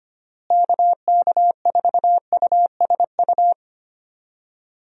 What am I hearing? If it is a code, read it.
Morse code, heard as a keyed tone, about 25 wpm, 700 Hz. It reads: KK4USU
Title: USA Call Signs - 6 Characters - Repeated 3x - 25wpm